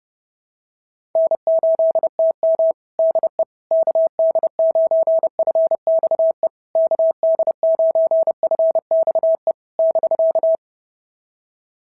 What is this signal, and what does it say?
Morse code, heard as a keyed tone, about 28 wpm, 650 Hz. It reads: N8TM DE KD9FXE KD9FXE <BK>
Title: SOTA QSO Practice 30wpm